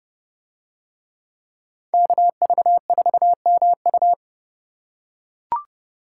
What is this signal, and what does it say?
Morse code, heard as a keyed tone, about 29 wpm, 700 Hz. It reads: KV4MU